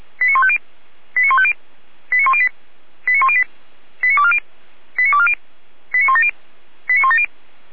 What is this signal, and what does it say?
Five-tone selective call (ZVEI), heard as quick runs of five tones, tones of about 70 ms. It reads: 89150 89150 89108 89108 89240 89240 89170 89170